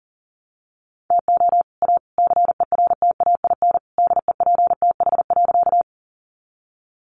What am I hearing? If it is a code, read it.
Morse code, heard as a keyed tone, about 40 wpm, 700 Hz. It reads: TO A CERTAIN DEPTH.
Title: Sentences from Top 1200 Words 40wpm